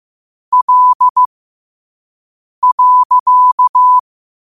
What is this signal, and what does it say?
Morse code, heard as a keyed tone, about 15 wpm, spaced as for 5 wpm, 1000 Hz. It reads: L.
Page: Quotes for Fri, 15 Aug 2025 in Morse Code at 5 words per minute.